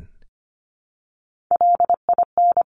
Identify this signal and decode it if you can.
Morse code, heard as a keyed tone, about 25 wpm, 700 Hz. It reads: LIN
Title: Top Three Letter Combinations in Words 25wpm